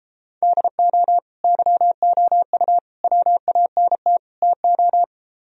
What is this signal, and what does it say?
Morse code, heard as a keyed tone, about 37 wpm, 700 Hz. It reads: DO YOU WANT TO